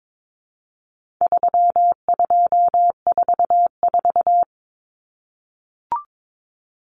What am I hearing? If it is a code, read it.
Morse code, heard as a keyed tone, about 22 wpm, 700 Hz. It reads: 3244
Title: Common Cut Numbers - 4-Digit 22wpm